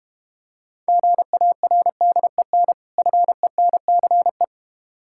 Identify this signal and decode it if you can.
Morse code, heard as a keyed tone, about 33 wpm, 700 Hz. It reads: GARDEN FENCE